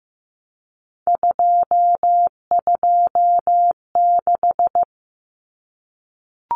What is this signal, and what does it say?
Morse code, heard as a keyed tone, about 15 wpm, 700 Hz. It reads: 226
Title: Numbers - 3-Digit 15wpm